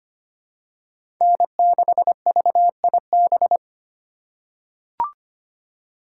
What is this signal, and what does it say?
Morse code, heard as a keyed tone, about 25 wpm, 700 Hz. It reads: N6VIB
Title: USA Call Signs - 5 Characters - 2x Character Spacing - 25wpm